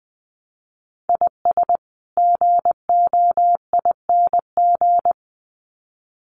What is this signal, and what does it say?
Morse code, heard as a keyed tone, about 20 wpm, 700 Hz. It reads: IS GOING
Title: Sets of 2 Words 20wpm